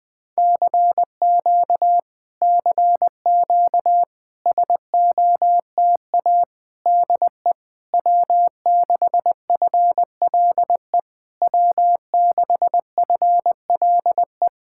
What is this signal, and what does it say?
Morse code, heard as a keyed tone, about 20 wpm, 700 Hz. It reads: CQ CQ SOTA DE W6FLE W6FLE
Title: Speed-Racing Common QSO Phrases - 17wpm